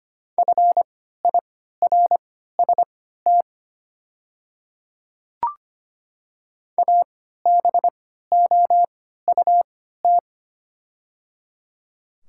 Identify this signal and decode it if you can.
Morse code, heard as a keyed tone, about 25 wpm, spaced as for 14 wpm, 700 Hz. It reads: FIRST ABOUT
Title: 3x Character Spacing - Top 100 Words - 25wpm